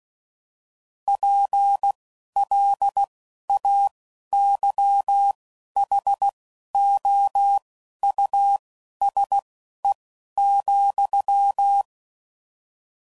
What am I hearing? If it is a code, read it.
Morse code, heard as a keyed tone, about 16 wpm, spaced as for 11 wpm, 800 Hz. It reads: PLAYHOUSE,
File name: Suffixes / General / Houses